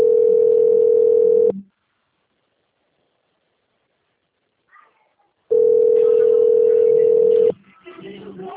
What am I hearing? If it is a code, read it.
Ringback tone (North America), 440 + 480 Hz.
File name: out- - - - - .4217